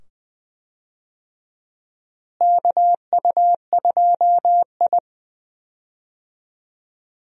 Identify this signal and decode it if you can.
Morse code, heard as a keyed tone, about 20 wpm, 700 Hz. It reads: KU2I
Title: USA Call Signs - 4 Characters - Repeated 2x - 20wpm